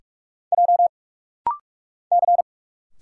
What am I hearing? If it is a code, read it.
Morse code, heard as a keyed tone, about 45 wpm, 700 Hz. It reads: J C
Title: Single Letters - Rapid-Fire 45wpm